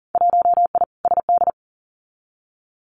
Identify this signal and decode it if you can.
Morse code, heard as a keyed tone, about 42 wpm, 700 Hz. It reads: 1I SD
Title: Winter Field Day Exchanges 40wpm